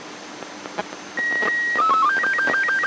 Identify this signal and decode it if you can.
SSTV (slow-scan television), Robot 36.
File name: Echo AO51 20-7-2009 1729z M0UKD